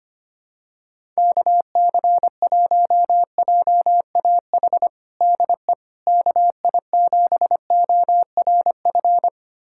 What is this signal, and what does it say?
Morse code, heard as a keyed tone, about 25 wpm, 700 Hz. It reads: KC1JAH DE KI7ORF
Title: USA Callsign DE Callsign 25wpm